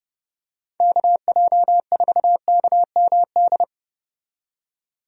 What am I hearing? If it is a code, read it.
Morse code, heard as a keyed tone, about 30 wpm, 700 Hz. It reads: KJ4KMD
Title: USA Call Signs - 6 Characters - 30wpm